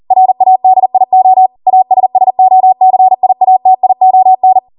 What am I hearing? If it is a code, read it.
Morse code, heard as a keyed tone, about 40 wpm, 750 Hz. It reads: RADIO ASSOCIATION